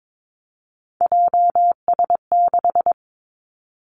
Morse code, heard as a keyed tone, about 22 wpm, 700 Hz. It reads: JS6
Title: Three Challenging Letter-Number Combinations 22wpm